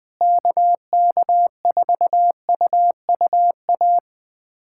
Morse code, heard as a keyed tone, about 21 wpm, 700 Hz. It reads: KK4UUA
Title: USA Callsign DE Callsign 20wpm